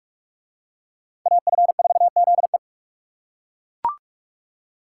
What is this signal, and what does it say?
Morse code, heard as a keyed tone, about 42 wpm, 700 Hz. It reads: AF4ZE